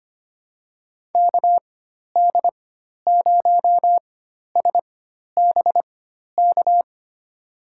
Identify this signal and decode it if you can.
Morse code, heard as a keyed tone, about 25 wpm, spaced as for 12 wpm, 700 Hz. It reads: KD0SBK
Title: USA Call Signs - 6 Characters - 4x Character Spacing - 25wpm